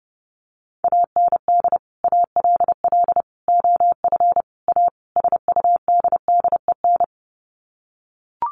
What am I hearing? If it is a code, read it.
Morse code, heard as a keyed tone, about 30 wpm, 700 Hz. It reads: AND ALL OF A SUDDEN
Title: Sets of 5 Words 30wpm